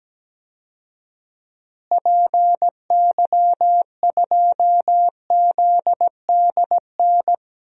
Morse code, heard as a keyed tone, about 17 wpm, 700 Hz. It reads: PY2ZDN
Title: DXCC-100 Call Signs - 6 Characters - Repeated 3x - 17wpm